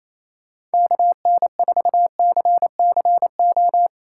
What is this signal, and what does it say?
Morse code, heard as a keyed tone, about 35 wpm, 700 Hz. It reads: KN4CCO
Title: Speed Racing - USA Call Signs 28wpm